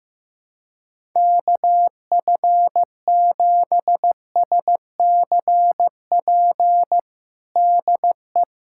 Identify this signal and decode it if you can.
Morse code, heard as a keyed tone, about 15 wpm, 700 Hz. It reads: KF7SCP DE
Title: USA Callsign DE Callsign - 6 Characters - 15wpm